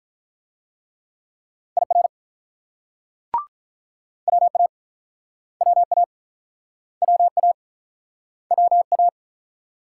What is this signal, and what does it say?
Morse code, heard as a keyed tone, about 48 wpm, 700 Hz. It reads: IR WA WA WA WA